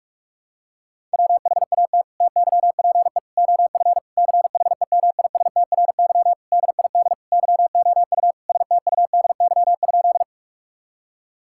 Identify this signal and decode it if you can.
Morse code, heard as a keyed tone, about 41 wpm, 700 Hz. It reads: WHAT TYPE OF CHEMISTRY DID YOU STUDY?
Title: Sentences from Top 1000 Words - Review Part 6 - 45wpm